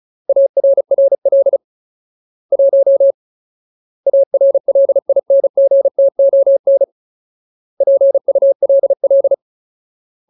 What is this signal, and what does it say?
Morse code, heard as a keyed tone, about 34 wpm, 550 Hz. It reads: ARRL 1 ARLINGTON PULL
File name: Ultimate Word List / 35WPM Ultimate